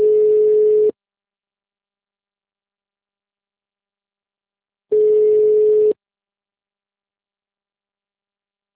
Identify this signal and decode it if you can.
Ringback tone (Europe, 425 Hz).